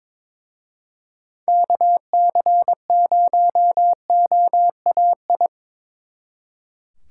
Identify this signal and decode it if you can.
Morse code, heard as a keyed tone, about 24 wpm, 700 Hz. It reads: KC0OAI
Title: Speed Racing - USA 2x3 Call Signs 22wpm